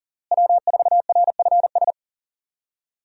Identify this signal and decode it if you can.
Morse code, heard as a keyed tone, about 37 wpm, 700 Hz. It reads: W4RFS
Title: USA Call Signs - 5 Characters - Repeated 2x - 40wpm